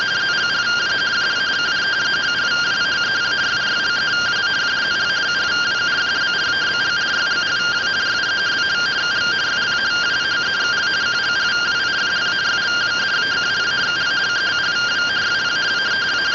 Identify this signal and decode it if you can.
Two-tone teleprinter signal, about 170 Hz shift.
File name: findings / CT 1 Scout 0001.gen-39 rtty rx